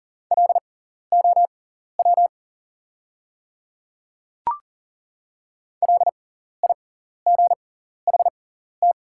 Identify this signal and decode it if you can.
Morse code, heard as a keyed tone, about 39 wpm, spaced as for 14 wpm, 690 Hz. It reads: LOW LIGHT
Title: 6x Character Spacing - Top 200 Words - 40wpm